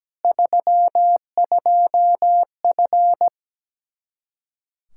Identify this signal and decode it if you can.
Morse code, heard as a keyed tone, about 17 wpm, 700 Hz. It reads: 32F
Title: Practice QSO - Exchange 2 - Info - 17wpm